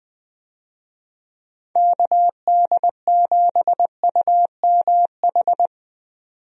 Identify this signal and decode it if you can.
Morse code, heard as a keyed tone, about 20 wpm, 700 Hz. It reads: KD7UMH
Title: USA Call Signs - 6 Characters - 20wpm